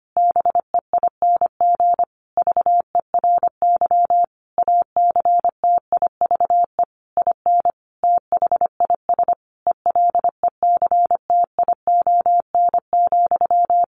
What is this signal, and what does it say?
Morse code, heard as a keyed tone, about 25 wpm, 700 Hz. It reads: BEING VERY ACTIVE IN THIS ELECTION,
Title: Sentences from Top 1500 Words 25wpm